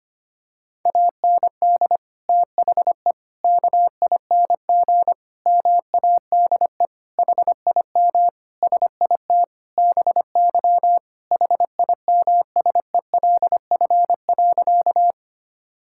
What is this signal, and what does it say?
Morse code, heard as a keyed tone, about 25 wpm, 700 Hz. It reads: AND THE KING MADE HIM SIT BY HIMSELF.
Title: Sentences from Top 1600 Words 25wpm